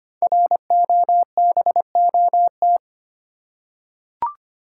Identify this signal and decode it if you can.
Morse code, heard as a keyed tone, about 25 wpm, 700 Hz. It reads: ROBOT
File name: 5 Letter Words - First 3 Letters Given - 25wpm